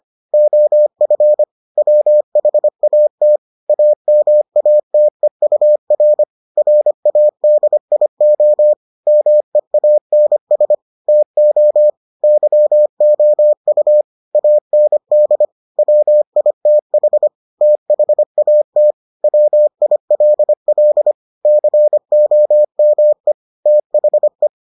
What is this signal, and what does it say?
Morse code, heard as a keyed tone, about 25 wpm, 600 Hz. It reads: OF WHAT AMATEUR RADIO MEANS TO YOU AND WITH THAT WILL COME THE